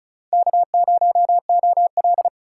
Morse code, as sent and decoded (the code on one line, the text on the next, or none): -.- ----- --- .-..
K0OL